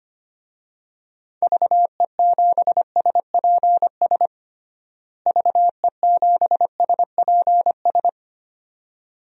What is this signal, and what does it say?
Morse code, heard as a keyed tone, about 25 wpm, 700 Hz. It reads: VE7SPS VE7SPS
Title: DXCC-100 Call Signs - 6 Characters - Repeated 2x - 25wpm